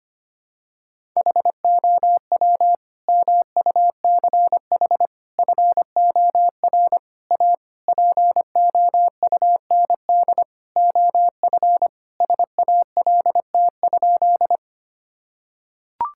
Morse code, heard as a keyed tone, about 25 wpm, 700 Hz. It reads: HOW MUCH FOR A POUND OF SALT?
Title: Sentences from Top 1000 Words - Review Part 5 - 25wpm